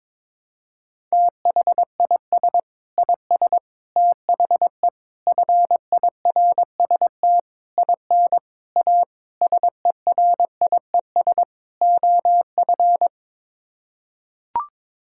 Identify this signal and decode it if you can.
Morse code, heard as a keyed tone, about 22 wpm, 700 Hz. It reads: THIS IS THE FIRST IN A SERIES OF
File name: practice / 22wpm / Sets of 8 Words - 22wpm